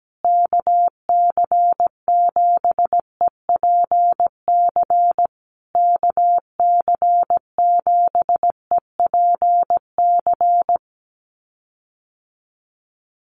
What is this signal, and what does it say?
Morse code, heard as a keyed tone, about 17 wpm, 700 Hz. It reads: KC7EPC KC7EPC
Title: USA Call Signs Repeated 3x 17wpm